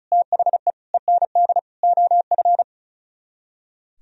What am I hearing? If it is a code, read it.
Morse code, heard as a keyed tone, about 47 wpm, spaced as for 41 wpm, 700 Hz. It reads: THE END OF
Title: Speed-Racing - Sets of 4 Words 35wpm